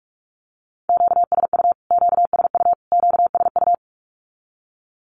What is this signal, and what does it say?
Morse code, heard as a keyed tone, about 44 wpm, 700 Hz. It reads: QSU QSU QSU
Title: Q-Codes Repeated 3x 45wpm